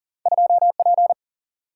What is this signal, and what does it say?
Morse code, heard as a keyed tone, about 39 wpm, 700 Hz. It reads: 2P
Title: Single-Letter Single-Number 40wpm